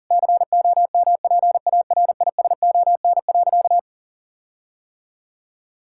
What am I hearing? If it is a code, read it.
Morse code, heard as a keyed tone, about 40 wpm, 700 Hz. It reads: COMPARISON.
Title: Sentences from Top 1600 Words 40wpm